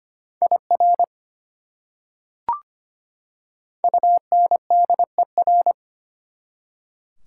Morse code, heard as a keyed tone, about 25 wpm, 700 Hz. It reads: IR UNDER
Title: Prefixes and Suffixes 25wpm